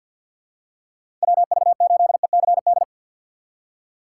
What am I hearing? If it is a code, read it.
Morse code, heard as a keyed tone, about 53 wpm, 700 Hz. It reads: WV8ECD
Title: USA Call Signs - 6 Characters - 4x Character Spacing - 50wpm